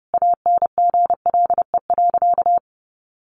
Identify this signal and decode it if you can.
Morse code, heard as a keyed tone, about 30 wpm, 700 Hz. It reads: ANGLE.
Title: Sentences from Top 1500 Words 30wpm